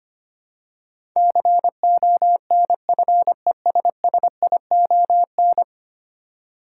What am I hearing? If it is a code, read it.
Morse code, heard as a keyed tone, about 25 wpm, 700 Hz. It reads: CONFESSION
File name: practice / 25wpm / Ten Letter Words - 25wpm